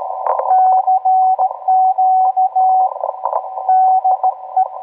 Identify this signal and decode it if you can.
Morse code, heard as a keyed tone, about 16 wpm, 750 Hz, with radio noise. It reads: CQ DE